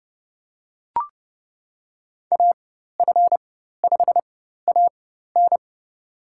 Morse code, heard as a keyed tone, about 30 wpm, spaced as for 14 wpm, 700 Hz. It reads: AF5AN
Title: USA Call Signs - 4x Character Spacing - 30wpm